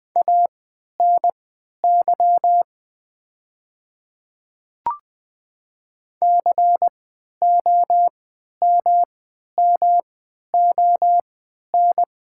Morse code, heard as a keyed tone, about 20 wpm, spaced as for 11 wpm, 700 Hz. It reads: ANY COMMON
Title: 3x Character Spacing - Top 500 Words - 20wpm